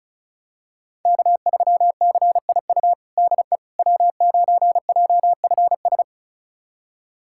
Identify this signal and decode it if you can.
Morse code, heard as a keyed tone, about 36 wpm, 700 Hz. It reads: K3CIU DE W9JFS
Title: USA Callsign DE Callsign - 5 Characters - 35wpm